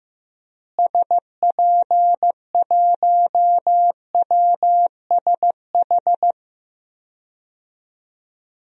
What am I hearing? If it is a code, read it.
Morse code, heard as a keyed tone, about 15 wpm, 700 Hz. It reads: SP1WSH
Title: DXCC-100 Call Signs - 6 Characters - Repeated 3x - 15wpm